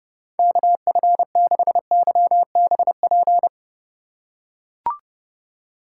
Morse code, heard as a keyed tone, about 30 wpm, 700 Hz. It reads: KF6YBP
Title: USA Call Signs - 6 Characters - 4x Character Spacing - 30wpm